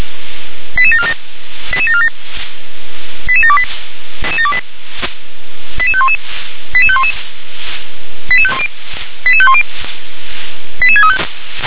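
Five-tone selective call (ZVEI), heard as quick runs of five tones, tones of about 70 ms. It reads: 88747 88747 88728 88511 88511 88510 88635
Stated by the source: Received by SDR; ZVEI five-tone call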